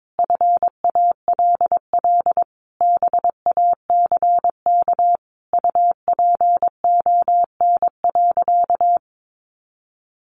Morse code, heard as a keyed tone, about 22 wpm, 700 Hz. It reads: FALL BACK UPON.
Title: Sentences from Top 1100 Words 22wpm